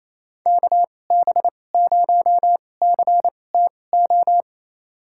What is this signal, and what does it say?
Morse code, heard as a keyed tone, about 28 wpm, spaced as for 20 wpm, 700 Hz. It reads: KB0CTO
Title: USA Call Signs - 6 Characters - 2x Character Spacing - 28wpm